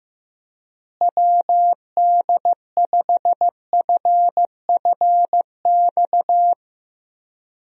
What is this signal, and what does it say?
Morse code, heard as a keyed tone, about 15 wpm, 700 Hz. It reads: WD5FFX
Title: USA Call Signs - 6 Characters - Repeated 3x - 15wpm